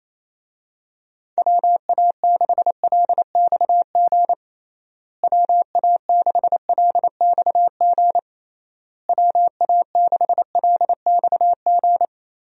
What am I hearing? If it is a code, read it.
Morse code, heard as a keyed tone, about 28 wpm, 700 Hz. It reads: WA6LXG WA6LXG WA6LXG
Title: USA Call Signs - 6 Characters - Repeated 3x - 28wpm